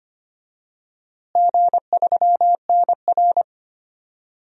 Morse code, heard as a keyed tone, about 25 wpm, 700 Hz. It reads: G3NR
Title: DXCC-100 Call Signs - 4 Characters - Repeated 3x - 25wpm